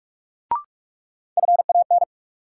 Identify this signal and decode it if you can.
Morse code, heard as a keyed tone, about 46 wpm, 700 Hz. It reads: FAN
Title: Sentences from Top 1300 Words - New Word Drill Alt - 45wpm